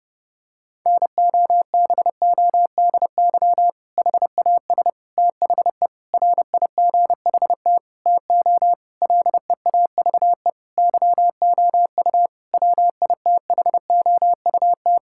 Morse code, heard as a keyed tone, about 30 wpm, 700 Hz. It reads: NOBODY HAS THE RIGHT TO LEAVE YOU WITHOUT
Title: Sentences from Top 1600 Words 30wpm